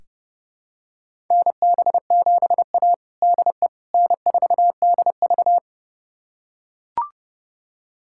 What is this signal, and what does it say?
Morse code, heard as a keyed tone, about 31 wpm, 700 Hz. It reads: NB7A DE N4DV
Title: USA Callsign DE Callsign - 4 Characters - 30wpm